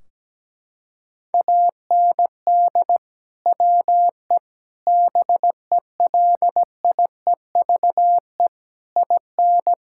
Morse code, heard as a keyed tone, about 17 wpm, 700 Hz. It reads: AND WE BELIEVE IN
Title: Sentences from Top 1600 Words 17wpm